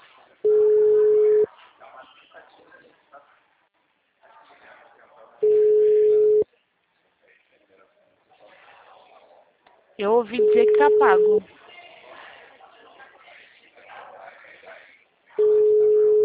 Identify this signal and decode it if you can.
Ringback tone (Europe, 425 Hz), with speech.